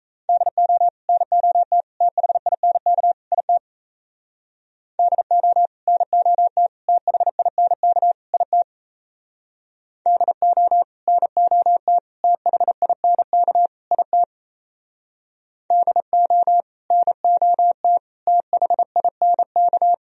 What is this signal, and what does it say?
Morse code, heard as a keyed tone, about 34 wpm, 700 Hz. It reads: DO NOT THINK IT DO NOT THINK IT DO NOT THINK IT DO NOT THINK
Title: Speed-Racing - Sets of 4 Words 28wpm